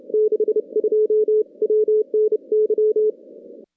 Morse code, heard as a keyed tone, about 29 wpm, 440 Hz. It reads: 62 WNY
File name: contests / CW-SS-k5tr- -2100